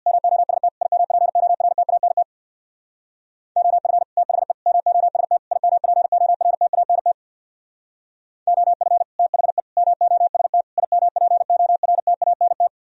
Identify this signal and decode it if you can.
Morse code, heard as a keyed tone, about 61 wpm, 700 Hz. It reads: MOST IMPORTANT OF THE MOST IMPORTANT OF THE MOST IMPORTANT
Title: Speed-Racing - Sets of 4 Words 50wpm